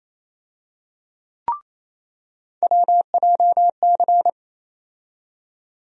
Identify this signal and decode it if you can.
Morse code, heard as a keyed tone, about 28 wpm, 700 Hz. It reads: WJC
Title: USA Call Sign Postfixes - 3 Letters 28wpm